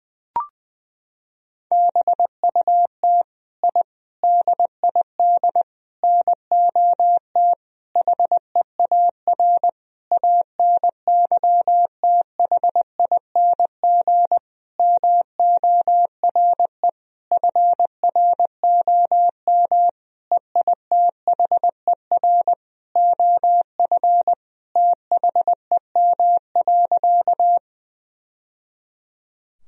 Morse code, heard as a keyed tone, about 21 wpm, 700 Hz. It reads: BUT I DID NOT HEAR ANYTHING MORE FROM EITHER OF THEM.
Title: Sentences from Top 1000 Words - Review Part 4 - 20wpm